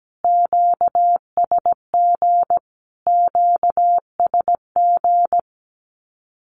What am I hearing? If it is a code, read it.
Morse code, heard as a keyed tone, about 17 wpm, 700 Hz. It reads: QSG QSG
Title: Q-Codes Repeated 3x 17wpm